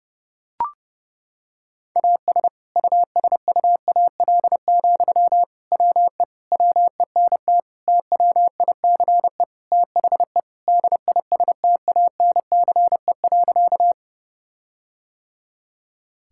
Morse code, heard as a keyed tone, about 30 wpm, 700 Hz. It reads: AS USUAL, WE WENT TWICE THE DISTANCE.
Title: Sentences from Top 1600 Words 30wpm